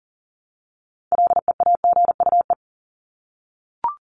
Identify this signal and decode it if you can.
Morse code, heard as a keyed tone, about 40 wpm, 700 Hz. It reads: LEAGUE
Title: Top 1751-2000 Words 40wpm